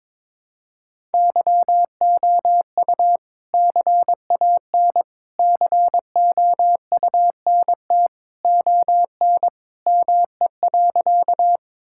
Morse code, heard as a keyed tone, about 22 wpm, 700 Hz. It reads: YOU CAN COUNT ON ME.
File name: All-Age Appropriate Jokes - 22wpm